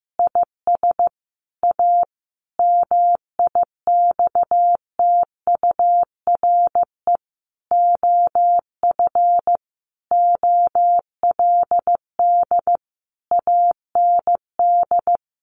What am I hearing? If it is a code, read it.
Morse code, heard as a keyed tone, about 15 wpm, 700 Hz. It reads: IS A MIXTURE OF OLD AND
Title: Sentences from Top 1500 Words 15wpm